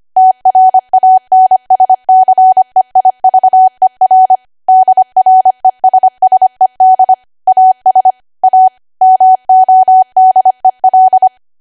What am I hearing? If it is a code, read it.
Morse code, heard as a keyed tone, about 25 wpm, 750 Hz. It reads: TRANSCEIVER DRESSED AS A MODEL